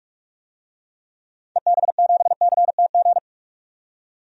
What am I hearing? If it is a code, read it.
Morse code, heard as a keyed tone, about 46 wpm, 700 Hz. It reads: EB7CTG
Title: DXCC-100 Call Signs - 6 Characters - 45wpm